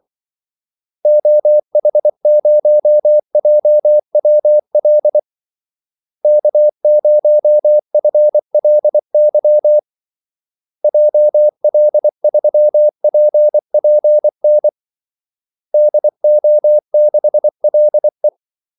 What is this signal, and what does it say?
Morse code, heard as a keyed tone, about 24 wpm, 600 Hz. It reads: OH0JWL K0FLY JL3PPN DO6LE
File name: Callsigns-24wpm